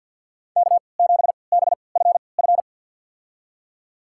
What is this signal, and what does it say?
Morse code, heard as a keyed tone, about 49 wpm, spaced as for 28 wpm, 690 Hz. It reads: K7BFF
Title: USA Call Signs - 5 Characters - 3x Character Spacing - 50wpm